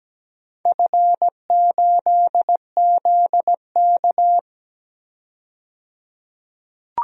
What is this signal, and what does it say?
Morse code, heard as a keyed tone, about 17 wpm, 700 Hz. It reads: F8ZK
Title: DXCC-100 Call Signs - 4 Characters - 17wpm